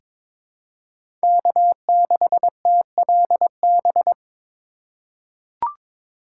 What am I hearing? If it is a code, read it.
Morse code, heard as a keyed tone, about 23 wpm, 700 Hz. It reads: K6TLB
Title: USA Call Signs - 5 Characters - 4x Character Spacing - 22wpm